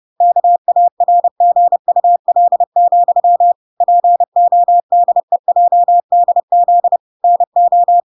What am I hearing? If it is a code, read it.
Morse code, heard as a keyed tone, about 30 wpm, 700 Hz. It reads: KARGUL, PODEJDZ NO